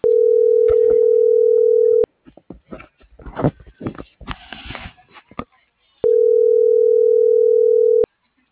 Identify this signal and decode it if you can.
Ringback tone (North America), 440 + 480 Hz.